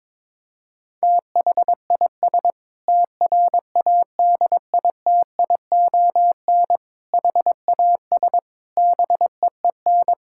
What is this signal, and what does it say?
Morse code, heard as a keyed tone, about 21 wpm, 700 Hz. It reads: THIS TRADITION HAS BEEN
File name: Sentences with 7-Word Phrases - 22wpm